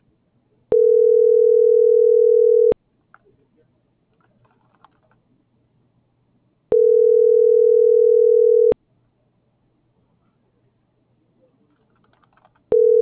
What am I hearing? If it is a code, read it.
Ringback tone (North America), 440 + 480 Hz.